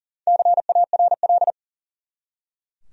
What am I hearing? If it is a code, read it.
Morse code, heard as a keyed tone, about 39 wpm, 700 Hz. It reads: CARL
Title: USA Common Names - 40wpm